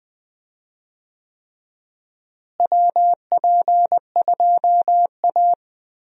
Morse code, heard as a keyed tone, about 20 wpm, 700 Hz. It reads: WP2A
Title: USA Call Signs - 4 Characters - Repeated 3x - 20wpm